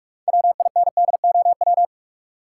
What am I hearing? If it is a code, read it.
Morse code, heard as a keyed tone, about 41 wpm, 700 Hz. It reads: WINDOW
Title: Sentences from Top 1600 Words - New Word Drill - 45wpm